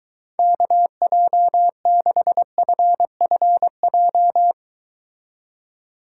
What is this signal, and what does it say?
Morse code, heard as a keyed tone, about 23 wpm, 700 Hz. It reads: KJ6FFJ